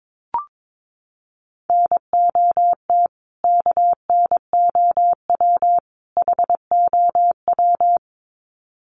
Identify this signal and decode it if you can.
Morse code, heard as a keyed tone, about 22 wpm, 700 Hz. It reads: NOT KNOW HOW
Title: Sets of 3 Words 22wpm